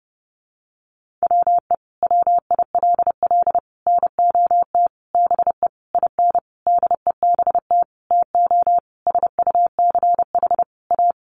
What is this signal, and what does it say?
Morse code, heard as a keyed tone, about 30 wpm, 700 Hz. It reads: WE WILL NOT BE IN DEBT TO SUCH A
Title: Sentences from Top 1300 Words 30wpm